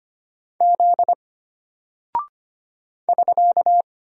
Morse code, heard as a keyed tone, about 25 wpm, 700 Hz. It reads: Z <SK>
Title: Single Character (Letter:Number:Punctuation:Prosign) Alt 25wpm